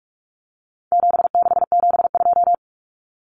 Morse code, heard as a keyed tone, about 44 wpm, 700 Hz. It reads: 7672
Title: Common Cut Numbers - 4-Digit 45wpm